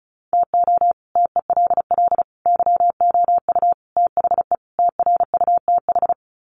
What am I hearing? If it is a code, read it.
Morse code, heard as a keyed tone, about 34 wpm, 700 Hz. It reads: TO TELL YOU THE TRUTH
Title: Sets of 5 Words 35wpm